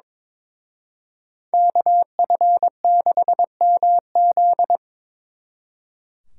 Morse code, heard as a keyed tone, about 25 wpm, 700 Hz. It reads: KF6MZ
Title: Speed Racing - USA 2x2 Call Signs 22wpm